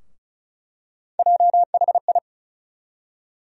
Morse code, heard as a keyed tone, about 36 wpm, 700 Hz. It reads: JHI
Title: USA Call Sign Postfixes - 3 Letters 35wpm